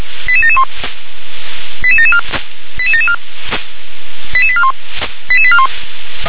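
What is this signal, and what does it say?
Five-tone selective call (ZVEI), heard as quick runs of five tones, tones of about 70 ms. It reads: 88771 88774 88774 88841 88841
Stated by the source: ZVEI five-tone selective call 88981 received on SDR.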